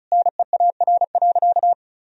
Morse code, heard as a keyed tone, about 36 wpm, 700 Hz. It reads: NEAR.
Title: Sentences from Top 1000 Words - Review Part 1 - 35wpm